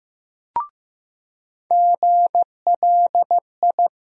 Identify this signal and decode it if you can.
Morse code, heard as a keyed tone, about 15 wpm, 700 Hz. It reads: GLI
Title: USA Call Sign Postfixes - 3 Letters 15wpm